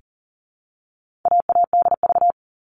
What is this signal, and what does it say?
Morse code, heard as a keyed tone, about 40 wpm, 700 Hz. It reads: AADV
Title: All Cut Numbers - 4-Digit 40wpm